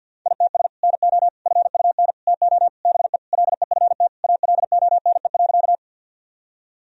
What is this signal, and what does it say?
Morse code, heard as a keyed tone, about 50 wpm, 700 Hz. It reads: ITS NO FUN TO BE LEFT ALONE.
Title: Sentences from Top 1000 Words - Review Part 4 - 50wpm